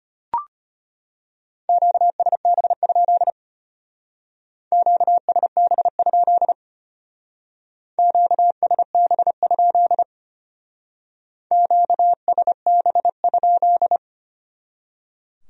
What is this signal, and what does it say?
Morse code, heard as a keyed tone, about 30 wpm, 700 Hz. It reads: QSB? QSB? QSB? QSB?
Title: Speed-Racing Q-Codes - 25wpm